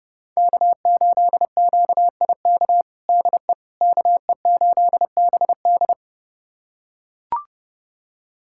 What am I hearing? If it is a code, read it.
Morse code, heard as a keyed tone, about 30 wpm, 700 Hz. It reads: K8QIK DE KE8BD
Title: USA Callsign DE Callsign - 5 Characters - 30wpm